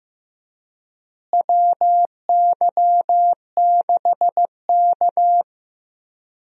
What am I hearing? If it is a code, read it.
Morse code, heard as a keyed tone, about 15 wpm, 700 Hz. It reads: WY6K
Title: USA Call Signs - 4 Characters - Repeated 3x - 15wpm